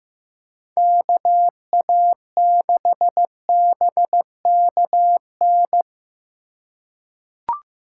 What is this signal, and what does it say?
Morse code, heard as a keyed tone, about 15 wpm, 700 Hz. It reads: KA6BKN
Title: USA Call Signs - 6 Characters - 2x Character Spacing - 15wpm